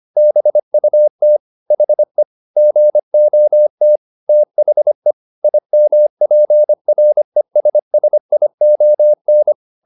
Morse code, heard as a keyed tone, about 25 wpm, 600 Hz. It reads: BUT HE GOT THE IMPRESSION